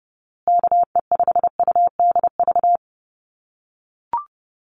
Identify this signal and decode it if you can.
Morse code, heard as a keyed tone, about 30 wpm, 700 Hz. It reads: KE5UDV
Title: USA Call Signs Repeated 2x 30wpm